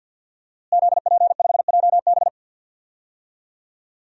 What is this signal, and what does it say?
Morse code, heard as a keyed tone, about 48 wpm, 700 Hz. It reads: ZP5JB
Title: DXCC-100 Call Signs - 5 Characters - Repeated 2x - 50wpm